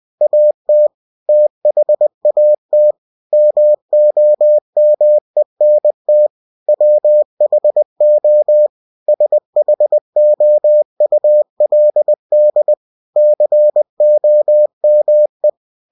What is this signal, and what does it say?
Morse code, heard as a keyed tone, about 20 wpm, 600 Hz. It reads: AT THAT MOMENT WHO SHOULD COME
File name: cw-books / winnie-the-pooh / winnie-20wpm-page-0066